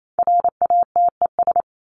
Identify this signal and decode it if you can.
Morse code, heard as a keyed tone, about 28 wpm, 700 Hz. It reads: RATES
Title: ES:S Suffix - Top 100 Words with Base Words 28wpm